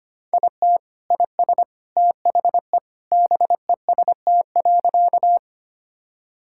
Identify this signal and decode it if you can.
Morse code, heard as a keyed tone, about 25 wpm, 700 Hz. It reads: IT IS THE BEST.
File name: practice / 25wpm / Sentences with 3-Word Phrases - 25wpm